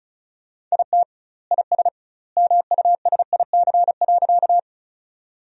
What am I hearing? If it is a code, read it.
Morse code, heard as a keyed tone, about 36 wpm, 700 Hz. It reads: IT IS MUSIC.
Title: 2x Word-Spacing - Sentences from Top 300 Words - 35wpm